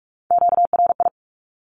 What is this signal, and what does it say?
Morse code, heard as a keyed tone, about 44 wpm, 700 Hz. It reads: QRI
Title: Q-Codes 45wpm